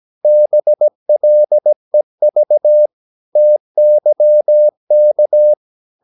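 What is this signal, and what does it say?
Morse code, heard as a keyed tone, about 17 wpm, 600 Hz. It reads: BLEV TYK
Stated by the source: Slow (QRS) Morse practice at 17 wpm, from CW med Gnister.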